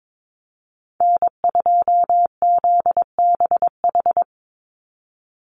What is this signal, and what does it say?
Morse code, heard as a keyed tone, about 22 wpm, 700 Hz. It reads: N2ZBH
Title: USA Call Signs 22wpm